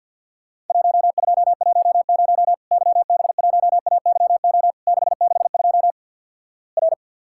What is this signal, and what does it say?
Morse code, heard as a keyed tone, about 50 wpm, 700 Hz. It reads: 1210 YB1AYO 662 R
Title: Worked All Europe DX Contest - QTC Exchange - 50wpm